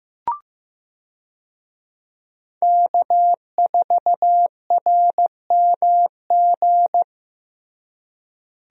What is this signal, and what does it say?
Morse code, heard as a keyed tone, about 15 wpm, 700 Hz. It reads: K4RMG
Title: USA Call Signs - 5 Characters - Repeated 3x - 15wpm